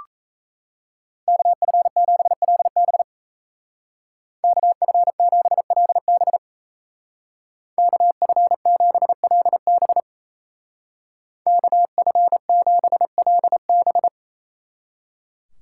Morse code, heard as a keyed tone, about 35 wpm, 700 Hz. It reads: KF7LB KF7LB KF7LB KF7LB
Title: Speed Racing - USA 2x2 Call Signs 28wpm